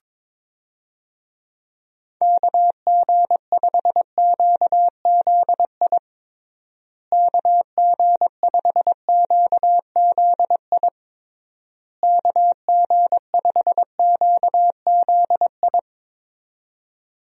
Morse code, heard as a keyed tone, about 22 wpm, 700 Hz. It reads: KG5QZI KG5QZI KG5QZI
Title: USA Call Signs - 6 Characters - Repeated 3x - 22wpm